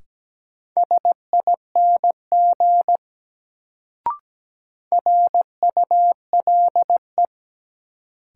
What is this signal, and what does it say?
Morse code, heard as a keyed tone, about 17 wpm, 700 Hz. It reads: SING RULE
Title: Sentences from Top 400 Words - New Word Drill - 17wpm